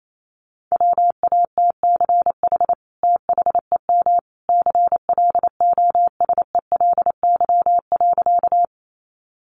Morse code, heard as a keyed tone, about 28 wpm, 700 Hz. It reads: WATCH THEM CLOSELY.
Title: Sentences from Top 1400 Words 28wpm